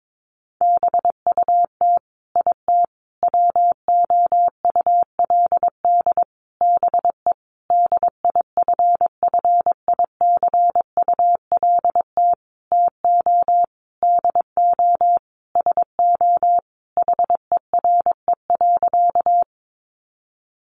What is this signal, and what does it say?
Morse code, heard as a keyed tone, about 22 wpm, 700 Hz. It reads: BUT IT WOULD BE DIFFICULT TO DO SO HERE.
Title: Sentences from Top 600 Words 22wpm